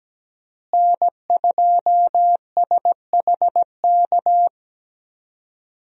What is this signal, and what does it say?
Morse code, heard as a keyed tone, about 17 wpm, 700 Hz. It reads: N2SHK
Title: USA Call Signs - 5 Characters - Repeated 3x - 17wpm